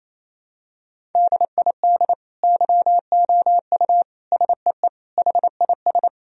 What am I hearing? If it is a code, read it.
Morse code, heard as a keyed tone, about 27 wpm, 700 Hz. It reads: DID YOU SEE HIS
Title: Sentences from Top 1000 Words - Review Part 1 - 28wpm